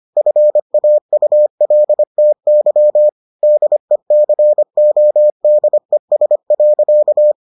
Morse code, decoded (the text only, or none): FAULTY DECODES.